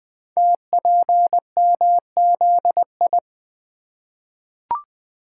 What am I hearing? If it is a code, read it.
Morse code, heard as a keyed tone, about 21 wpm, 700 Hz. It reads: TPMZI
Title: Code Groups - 5 Letters - 20wpm